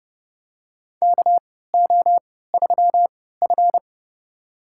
Morse code, heard as a keyed tone, about 28 wpm, spaced as for 17 wpm, 700 Hz. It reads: KO3F
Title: USA Call Signs - 4 Characters - 3x Character Spacing - 30wpm